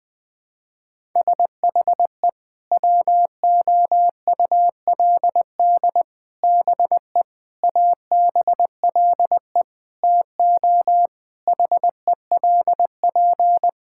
Morse code, heard as a keyed tone, about 21 wpm, 700 Hz. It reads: SHE WOULD BE ABLE TO HELP
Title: Sentences with 4-Word Phrases - 20wpm